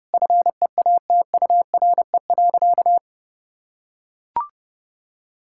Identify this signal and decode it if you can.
Morse code, heard as a keyed tone, about 32 wpm, 700 Hz. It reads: FEATURE.
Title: Sentences from Top 1000 Words - Review Part 5 - 30wpm